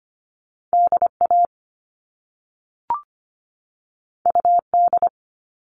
Morse code, heard as a keyed tone, about 25 wpm, 700 Hz. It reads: DA UD
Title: All Cut Numbers - 2-Digit 25wpm